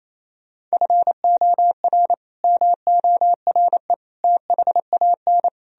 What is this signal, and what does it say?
Morse code, heard as a keyed tone, about 37 wpm, 700 Hz. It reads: FOR MORE THAN